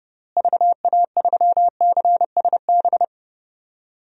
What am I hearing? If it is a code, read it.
Morse code, heard as a keyed tone, about 32 wpm, 700 Hz. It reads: VA3CSB
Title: DXCC-100 Call Signs - 6 Characters - Repeated 3x - 30wpm